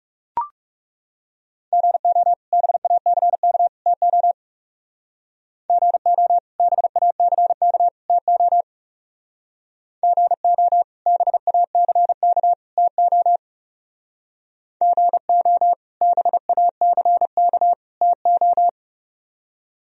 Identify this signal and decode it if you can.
Morse code, heard as a keyed tone, about 38 wpm, 700 Hz. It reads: GO BACK TO GO BACK TO GO BACK TO GO BACK TO